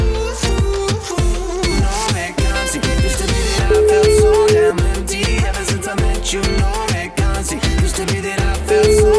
Ringback tone (Europe, 425 Hz), with noise.